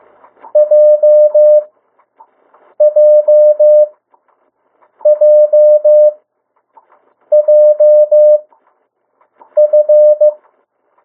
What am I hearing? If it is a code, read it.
Morse code, heard as a keyed tone, about 15 wpm, spaced as for 6 wpm, 600 Hz. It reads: JJJJF
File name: morse mp3 / 02 urok 1